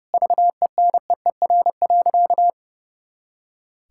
Morse code, heard as a keyed tone, about 30 wpm, 700 Hz. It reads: VENEER.
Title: Pangram Challenge v2 - 30wpm